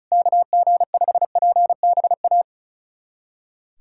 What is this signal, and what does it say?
Morse code, heard as a keyed tone, about 36 wpm, 690 Hz. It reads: KG5PBA
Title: USA Callsign DE Callsign - 6 Characters - 35wpm